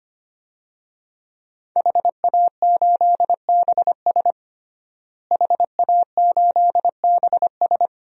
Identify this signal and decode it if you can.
Morse code, heard as a keyed tone, about 25 wpm, 700 Hz. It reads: HA8BS HA8BS
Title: DXCC-100 Call Signs - 5 Characters - Repeated 2x - 25wpm